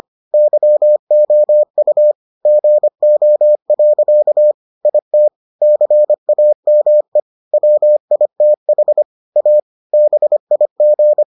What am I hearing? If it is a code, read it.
Morse code, heard as a keyed tone, about 25 wpm, 600 Hz. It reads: YOU GO. IT CAME WITH A BIG